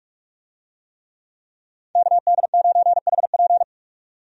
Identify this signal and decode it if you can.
Morse code, heard as a keyed tone, about 46 wpm, 700 Hz. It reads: KD9HP
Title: USA Call Signs - 5 Characters - 45wpm